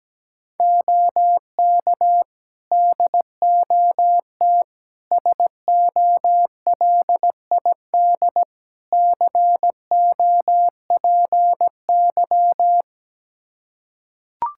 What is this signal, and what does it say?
Morse code, heard as a keyed tone, about 17 wpm, 700 Hz. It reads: OK DOT SOLID COPY
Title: Practice QSO - Exchange 3 - More Info - 17wpm